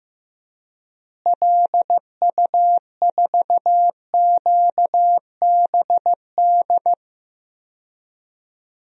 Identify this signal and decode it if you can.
Morse code, heard as a keyed tone, about 15 wpm, 700 Hz. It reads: LU4QBD
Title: DXCC-100 Call Signs - 6 Characters - Repeated 3x - 15wpm